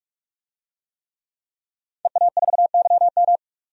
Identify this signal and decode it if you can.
Morse code, heard as a keyed tone, about 42 wpm, 700 Hz. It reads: EA4YM
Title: DXCC-100 Call Signs - 5 Characters - 45wpm